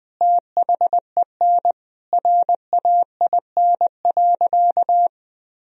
Morse code, heard as a keyed tone, about 20 wpm, 700 Hz. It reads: THEN RAIN.
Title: Sentences from Top 1000 Words - Review Part 2 - 20wpm